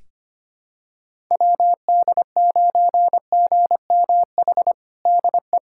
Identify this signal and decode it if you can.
Morse code, heard as a keyed tone, about 25 wpm, 700 Hz. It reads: WD9GMH DE
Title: USA Callsign DE Callsign 25wpm